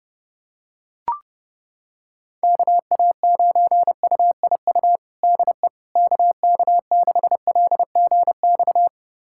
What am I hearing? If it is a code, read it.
Morse code, heard as a keyed tone, about 30 wpm, 700 Hz. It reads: KA9UIU DE KK6LGX
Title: USA Callsign DE Callsign - 6 Characters - 30wpm